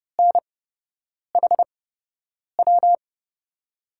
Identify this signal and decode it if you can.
Morse code, heard as a keyed tone, about 30 wpm, spaced as for 8 wpm, 700 Hz. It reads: NHW
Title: USA Call Signs - 8x Character Spacing - 30wpm